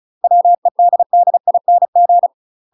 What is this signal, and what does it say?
Morse code, heard as a keyed tone, about 34 wpm, 700 Hz. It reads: WEDDING
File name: UK MASTER QSOs TRAINING / 35WPM / UK MASTER QSOs 0034